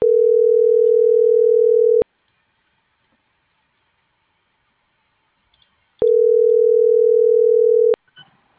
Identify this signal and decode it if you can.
Ringback tone (North America), 440 + 480 Hz.
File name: CENTRAL MAYAB / out jorge.armando